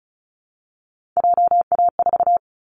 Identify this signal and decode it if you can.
Morse code, heard as a keyed tone, about 34 wpm, 700 Hz. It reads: JA4
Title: Callsign Prefixes for Top 25 DXCC - Repeated 3x - 35wpm